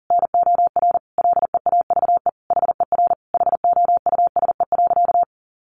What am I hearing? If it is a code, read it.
Morse code, heard as a keyed tone, about 40 wpm, 700 Hz. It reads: NOR LEAVE HER HOUSE.
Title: Sentences from Top 1500 Words 40wpm